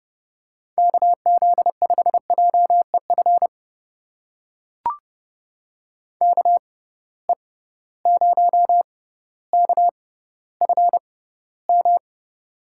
Morse code, heard as a keyed tone, about 30 wpm, spaced as for 15 wpm, 700 Hz. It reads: KZ5JEF KE0KFM
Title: USA Call Signs - 6 Characters - 6x Character Spacing - 30wpm